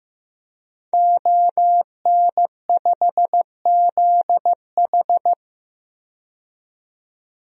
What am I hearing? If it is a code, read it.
Morse code, heard as a keyed tone, about 15 wpm, 700 Hz. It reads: ON5ZH
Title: DXCC-100 Call Signs - 5 Characters - 15wpm